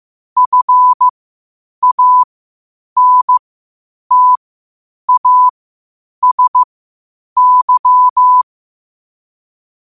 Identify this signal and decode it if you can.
Morse code, heard as a keyed tone, about 15 wpm, spaced as for 8 wpm, 1000 Hz. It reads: FANTASY